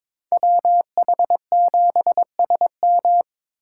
Morse code, heard as a keyed tone, about 26 wpm, 700 Hz. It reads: WH7SM
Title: Speed Racing - USA 2x2 Call Signs 22wpm